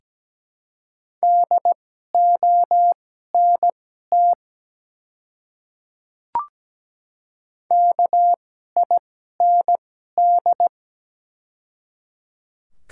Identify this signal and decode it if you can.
Morse code, heard as a keyed tone, about 17 wpm, spaced as for 12 wpm, 700 Hz. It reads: DONT KIND
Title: 2x Character Spacing - Top 300 Words - 17wpm